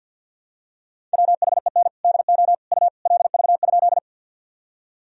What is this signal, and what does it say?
Morse code, heard as a keyed tone, about 46 wpm, 700 Hz. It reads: WHEN DO U LV?
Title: Sentences from Top 400 Words and Abbreviations 50wpm